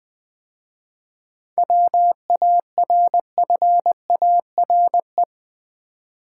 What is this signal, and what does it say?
Morse code, heard as a keyed tone, about 20 wpm, 690 Hz. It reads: WARFARE
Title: 7 Letter Words - First 3 Letters Given - 20wpm